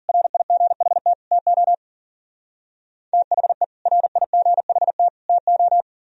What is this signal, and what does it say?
Morse code, heard as a keyed tone, about 54 wpm, 700 Hz. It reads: RIGHT TO THE RIGHT TO